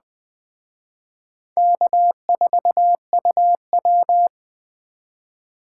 Morse code, heard as a keyed tone, about 20 wpm, 700 Hz. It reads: K4UW
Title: Speed Racing - USA 1x2 Call Signs 17wpm